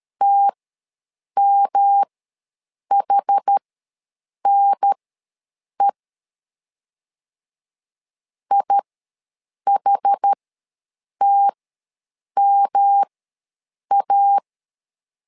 Morse code, heard as a keyed tone, about 13 wpm, spaced as for 7 wpm, 800 Hz. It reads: TMHNE IHTMA
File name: LEZIONI / 3 MWRO / 3 M ws 12wpm 1745